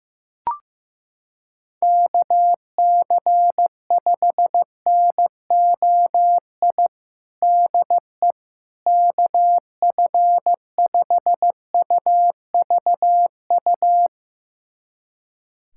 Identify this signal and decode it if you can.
Morse code, heard as a keyed tone, about 15 wpm, 700 Hz. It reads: KC5NOI DE KF5UVU
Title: USA Callsign DE Callsign - 6 Characters - 15wpm